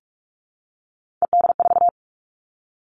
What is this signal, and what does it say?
Morse code, heard as a keyed tone, about 44 wpm, 700 Hz. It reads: ED4
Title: All Cut Numbers - 3-Digit 45wpm